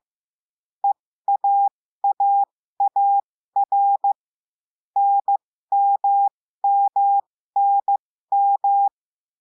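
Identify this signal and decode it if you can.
Morse code, heard as a keyed tone, about 15 wpm, spaced as for 13 wpm, 800 Hz. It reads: EAAAR NMMNM
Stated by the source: Morse lesson adding the letter T, with five-letter practice groups.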